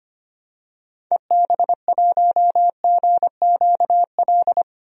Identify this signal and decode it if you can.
Morse code, heard as a keyed tone, about 25 wpm, 700 Hz. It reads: EB1GQL
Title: DXCC-100 Call Signs - 6 Characters - 25wpm